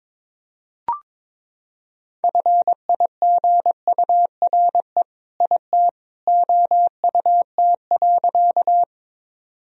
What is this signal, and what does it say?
Morse code, heard as a keyed tone, about 22 wpm, 700 Hz. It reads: FIGURE IT OUT.
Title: Sentences from Top 1000 Words - Review Part 2 - 22wpm